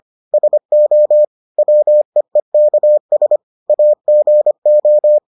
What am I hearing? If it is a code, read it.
Morse code, heard as a keyed tone, about 25 wpm, 600 Hz. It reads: SO WEEKS AGO